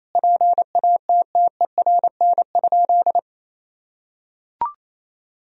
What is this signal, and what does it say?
Morse code, heard as a keyed tone, about 28 wpm, 700 Hz. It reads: PATTERN?
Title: Sentences from Top 1000 Words - Review Part 2 - 28wpm